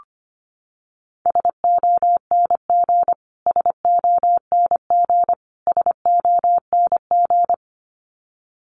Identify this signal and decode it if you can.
Morse code, heard as a keyed tone, about 25 wpm, 700 Hz. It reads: SONG SONG SONG
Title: Top 500 Words Repeated 25wpm